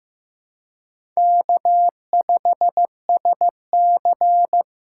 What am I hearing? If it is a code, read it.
Morse code, heard as a keyed tone, about 15 wpm, 700 Hz. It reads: K5SC
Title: USA Call Signs - 2x Character Spacing - 15wpm